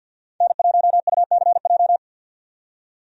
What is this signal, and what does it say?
Morse code, heard as a keyed tone, about 48 wpm, 700 Hz. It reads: N1UCJ
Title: USA Call Signs - 5 Characters - 50wpm